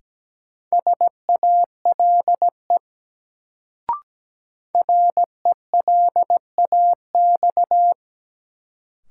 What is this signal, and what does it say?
Morse code, heard as a keyed tone, about 17 wpm, 700 Hz. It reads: SALE RELAX
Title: Sentences from Top 1400 Words - New Word Drill Alt - 17wpm